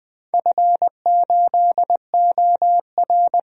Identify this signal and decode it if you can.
Morse code, heard as a keyed tone, about 20 wpm, 700 Hz. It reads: F8OR
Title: DXCC-100 Call Signs - 4 Characters - 20wpm